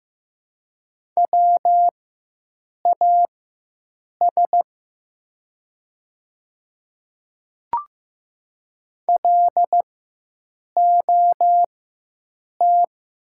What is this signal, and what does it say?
Morse code, heard as a keyed tone, about 15 wpm, spaced as for 7 wpm, 700 Hz. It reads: WAS LOT
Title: 4x Character Spacing - Top 500 Words - 15wpm